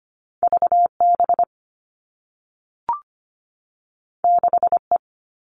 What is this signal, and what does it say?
Morse code, heard as a keyed tone, about 25 wpm, 700 Hz. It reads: V B 6 E
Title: All Cut Numbers - 2-Digit 25wpm